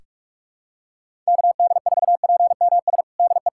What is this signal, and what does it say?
Morse code, heard as a keyed tone, about 46 wpm, 700 Hz. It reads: KD4PMS DE
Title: USA Callsign DE Callsign 45wpm